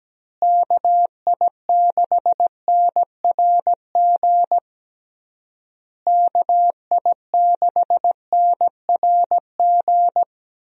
Morse code, heard as a keyed tone, about 17 wpm, 700 Hz. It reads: KI6NRG KI6NRG
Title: USA Call Signs - 6 Characters - Repeated 3x - 17wpm